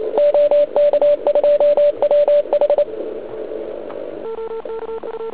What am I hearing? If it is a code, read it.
Morse code, heard as a keyed tone, about 28 wpm, 580 Hz. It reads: OK2WH
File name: ARCHIV / zvuk2 / CW RUN